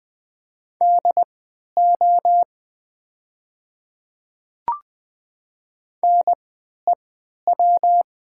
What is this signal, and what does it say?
Morse code, heard as a keyed tone, about 20 wpm, spaced as for 11 wpm, 700 Hz. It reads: DO NEW
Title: 3x Character Spacing - Top 300 Words - 20wpm